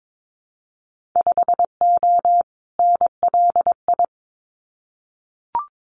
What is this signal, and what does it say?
Morse code, heard as a keyed tone, about 22 wpm, 700 Hz. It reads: 5O NLI
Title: Winter Field Day Exchanges 22wpm